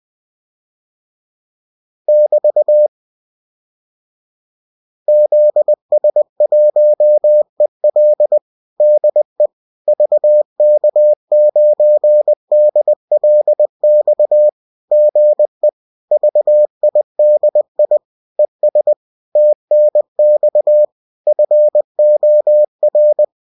Morse code, heard as a keyed tone, about 20 wpm, 600 Hz. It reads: <BT> ZS1EL DE VK9DLX GE VIDI ES TNX FOR